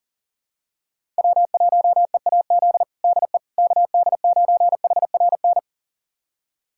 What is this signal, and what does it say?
Morse code, heard as a keyed tone, about 40 wpm, 700 Hz. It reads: W1EAZ DE KD9HRN
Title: USA Callsign DE Callsign 40wpm